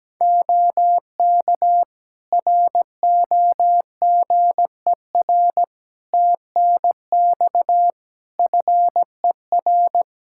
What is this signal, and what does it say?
Morse code, heard as a keyed tone, about 17 wpm, 700 Hz. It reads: OK ROGER TNX FER
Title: Practice QSO - Exchange 4 - Ending - 17wpm